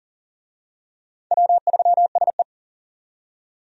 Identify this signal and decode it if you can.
Morse code, heard as a keyed tone, about 40 wpm, 700 Hz. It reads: W3SE
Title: USA Call Signs - 4 Characters - 40wpm